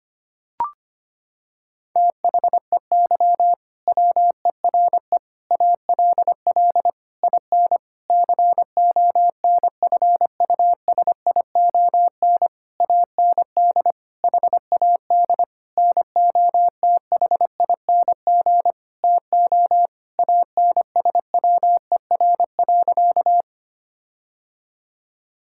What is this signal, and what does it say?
Morse code, heard as a keyed tone, about 25 wpm, 700 Hz. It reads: THEY WERE ALL IN CONFUSION AND HAD NOTHING TO ANSWER.
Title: Sentences from Top 1600 Words 25wpm